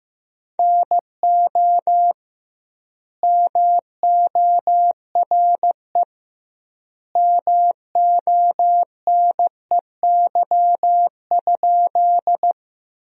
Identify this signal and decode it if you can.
Morse code, heard as a keyed tone, about 16 wpm, 700 Hz. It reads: NO MORE MONEY?
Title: 2x Word-Spacing - Sentences from Top 400 Words - 15wpm